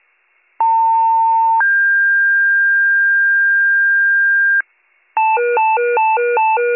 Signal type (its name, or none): two-tone page